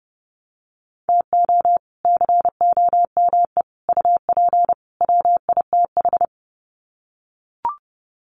Morse code, heard as a keyed tone, about 30 wpm, 700 Hz. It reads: TO COME UP WITH
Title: Sets of 4 Words 30wpm